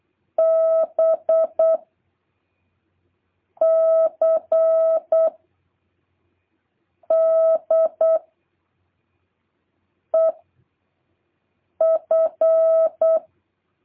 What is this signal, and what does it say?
Morse code, heard as a keyed tone, about 8 wpm, 660 Hz. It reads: B C D E F